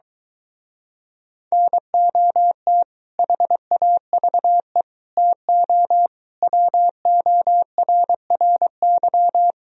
Morse code, heard as a keyed tone, about 23 wpm, 700 Hz. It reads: NOT HAVE TO WORRY